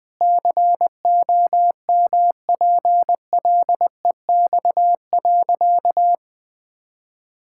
Morse code, heard as a keyed tone, about 19 wpm, 700 Hz. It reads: COMPLEX.
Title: Sentences from Top 1000 Words - Review Part 6 - 20wpm